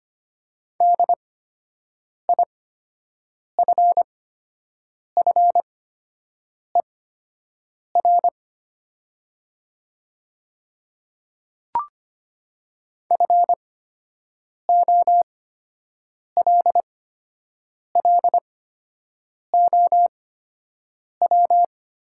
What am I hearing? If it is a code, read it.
Morse code, heard as a keyed tone, about 25 wpm, spaced as for 7 wpm, 700 Hz. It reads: DIFFER FOLLOW
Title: 8x Character Spacing - Top 200 Words - 25wpm